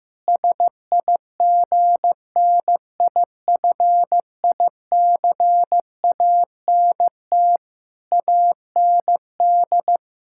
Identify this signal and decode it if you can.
Morse code, heard as a keyed tone, about 15 wpm, 700 Hz. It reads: SIGNIFICANT AND
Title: Sentences from Top 1000 Words - Review Part 6 - 15wpm